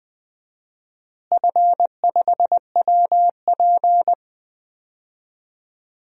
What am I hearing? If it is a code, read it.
Morse code, heard as a keyed tone, about 20 wpm, 700 Hz. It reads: F5WP
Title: DXCC-100 Call Signs - 4 Characters - Repeated 3x - 20wpm